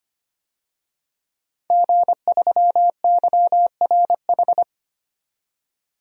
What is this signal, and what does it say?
Morse code, heard as a keyed tone, about 25 wpm, 700 Hz. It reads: G3YRH
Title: DXCC-100 Call Signs - 5 Characters - 25wpm